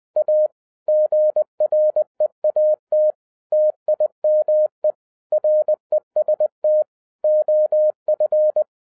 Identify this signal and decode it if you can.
Morse code, heard as a keyed tone, about 20 wpm, 610 Hz. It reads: A GREAT TIME REST OF
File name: 2011-07-mp3 / n48 JOHN K3TN 20wpm